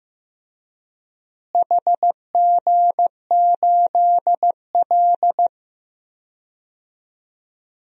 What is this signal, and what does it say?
Morse code, heard as a keyed tone, about 15 wpm, 700 Hz. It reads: HG8L
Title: DXCC-100 Call Signs - 4 Characters - Repeated 3x - 15wpm